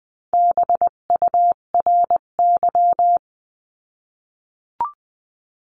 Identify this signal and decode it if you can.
Morse code, heard as a keyed tone, about 20 wpm, 700 Hz. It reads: BURY
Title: 258 Lesson 42c - Words 16-20wpm